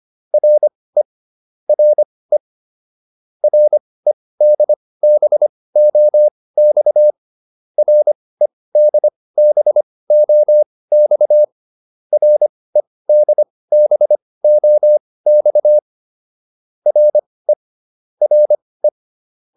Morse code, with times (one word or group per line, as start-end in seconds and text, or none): 0.3-1.0 RE
1.7-2.4 RE
3.4-7.1 REDBOX
7.8-11.4 REDBOX
12.1-15.8 REDBOX
16.9-17.5 RE
18.2-18.9 RE